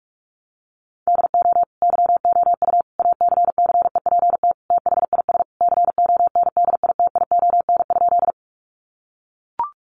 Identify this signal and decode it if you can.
Morse code, heard as a keyed tone, about 44 wpm, 700 Hz. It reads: DO YOU ACCEPT THIS CONDITION?
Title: Sentences from Top 1300 Words 45wpm